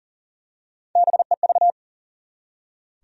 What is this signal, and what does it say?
Morse code, heard as a keyed tone, about 39 wpm, 700 Hz. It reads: BEV
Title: USA Common Names - Max 5 Letters - 40wpm